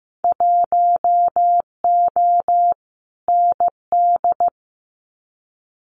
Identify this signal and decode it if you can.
Morse code, heard as a keyed tone, about 15 wpm, 700 Hz. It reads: 1O ND
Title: Winter Field Day Exchanges 15wpm